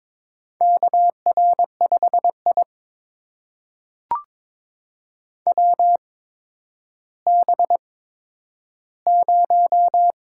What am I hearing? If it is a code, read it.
Morse code, heard as a keyed tone, about 22 wpm, spaced as for 11 wpm, 700 Hz. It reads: KR5I WB0
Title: USA Call Signs - 4 Characters - 8x Character Spacing - 22wpm